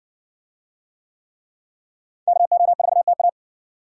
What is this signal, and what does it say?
Morse code, heard as a keyed tone, about 58 wpm, 700 Hz. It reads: KO4TA